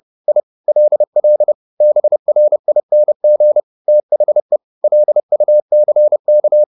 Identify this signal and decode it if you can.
Morse code, heard as a keyed tone, about 30 wpm, 600 Hz. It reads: I LL BRING THE LUCK